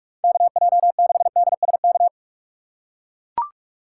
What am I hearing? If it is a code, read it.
Morse code, heard as a keyed tone, about 44 wpm, 700 Hz. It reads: KJ6DSK
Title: USA Call Signs - 6 Characters - 6x Character Spacing - 45wpm